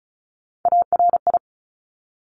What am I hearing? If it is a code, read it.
Morse code, heard as a keyed tone, about 36 wpm, 700 Hz. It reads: ARI
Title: Top Three Letter Combinations in Words 35wpm